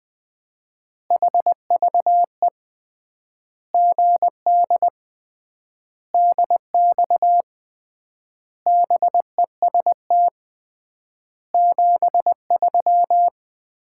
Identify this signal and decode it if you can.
Morse code, heard as a keyed tone, about 20 wpm, 700 Hz. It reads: HVE GD DX BEST 73
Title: Common QSO Phrases - 3x Word-Spacing - 20wpm